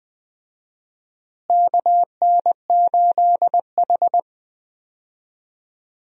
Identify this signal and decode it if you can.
Morse code, heard as a keyed tone, about 20 wpm, 700 Hz. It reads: KN8H
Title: USA Call Signs - 4 Characters - Repeated 2x - 20wpm